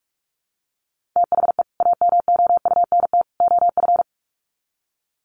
Morse code, heard as a keyed tone, about 44 wpm, 700 Hz. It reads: THE AMOUNT OF
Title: Sets of 3 Words 45wpm